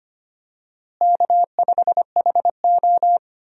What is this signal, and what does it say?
Morse code, heard as a keyed tone, about 25 wpm, 700 Hz. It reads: K5HO
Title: USA Call Signs - 4 Characters - Repeated 3x - 25wpm